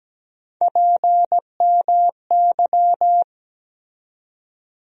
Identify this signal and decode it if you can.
Morse code, heard as a keyed tone, about 17 wpm, 700 Hz. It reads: PMY
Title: USA Call Sign Postfixes - 3 Letters 17wpm